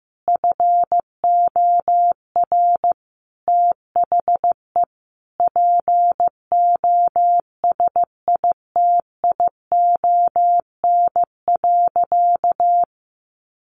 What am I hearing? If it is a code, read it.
Morse code, heard as a keyed tone, about 15 wpm, 700 Hz. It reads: FOR THE POSITION.
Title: Sentences from Top 1400 Words 15wpm